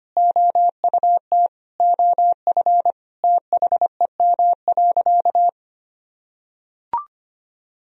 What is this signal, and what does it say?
Morse code, heard as a keyed tone, about 25 wpm, 700 Hz. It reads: OUT OF THEM.
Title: Sentences from Top 1000 Words - Review Part 3 - 25wpm